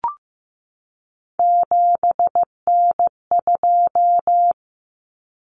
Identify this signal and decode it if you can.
Morse code, heard as a keyed tone, about 15 wpm, 700 Hz. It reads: 7N2
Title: Common Cut Numbers - 3-Digit 15wpm